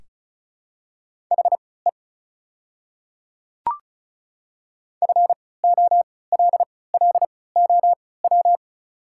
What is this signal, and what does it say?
Morse code, heard as a keyed tone, about 36 wpm, spaced as for 20 wpm, 690 Hz. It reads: HE FOLLOW
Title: 3x Character Spacing - Top 300 Words - 35wpm